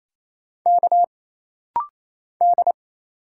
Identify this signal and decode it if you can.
Morse code, heard as a keyed tone, about 27 wpm, 700 Hz. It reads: K D
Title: Single Letter-Number - Rapid-Fire 28wpm